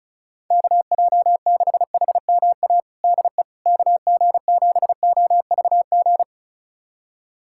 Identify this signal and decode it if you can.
Morse code, heard as a keyed tone, about 36 wpm, 690 Hz. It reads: KJ6HMA DE KG7OVG
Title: USA Callsign DE Callsign - 6 Characters - 35wpm